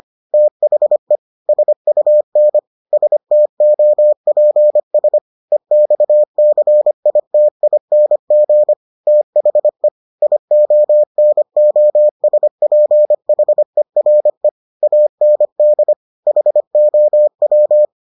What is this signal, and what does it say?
Morse code, heard as a keyed tone, about 25 wpm, 600 Hz. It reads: THE SUN STOPS EXCITING THE IONOSPHERE AND HOW